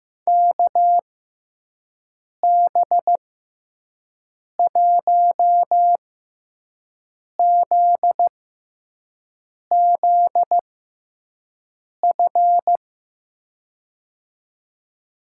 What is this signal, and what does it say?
Morse code, heard as a keyed tone, about 15 wpm, spaced as for 5 wpm, 700 Hz. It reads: KB1ZZF
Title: USA Call Signs - 6 Characters - 6x Character Spacing - 15wpm